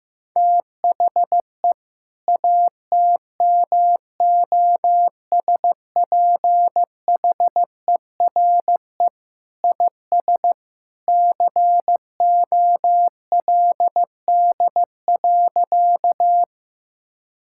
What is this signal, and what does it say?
Morse code, heard as a keyed tone, about 15 wpm, 700 Hz. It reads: THE ATMOSPHERE IS COLD.
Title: Sentences from Top 1600 Words 15wpm